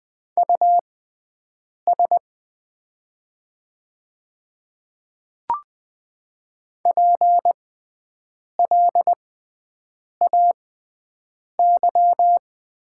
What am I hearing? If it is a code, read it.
Morse code, heard as a keyed tone, about 20 wpm, spaced as for 7 wpm, 700 Hz. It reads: US PLAY
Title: 6x Character Spacing - Top 200 Words - 20wpm